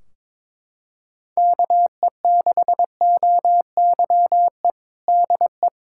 Morse code, heard as a keyed tone, about 22 wpm, 700 Hz. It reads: KE6OYE DE
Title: Practice QSO - Exchange 4 - Ending - 22wpm